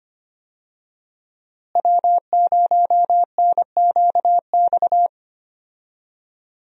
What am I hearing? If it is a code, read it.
Morse code, heard as a keyed tone, about 25 wpm, 700 Hz. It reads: W0NQX
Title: USA Call Signs - 5 Characters - Repeated 2x - 25wpm